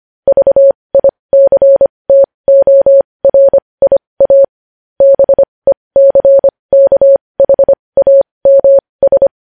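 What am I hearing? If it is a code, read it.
Morse code, heard as a keyed tone, about 25 wpm, spaced as for 20 wpm, 550 Hz. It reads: VICTORIA BECKHAMS
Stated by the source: Morse code news headlines sent at 20.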